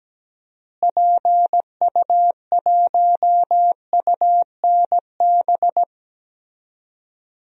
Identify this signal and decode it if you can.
Morse code, heard as a keyed tone, about 17 wpm, 700 Hz. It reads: PU1UNB
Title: DXCC-100 Call Signs - 6 Characters - Repeated 2x - 17wpm